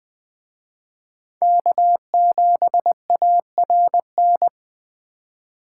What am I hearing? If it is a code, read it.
Morse code, heard as a keyed tone, about 23 wpm, 700 Hz. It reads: K7ARN
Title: Speed Racing - USA 1x3 Call Signs 20wpm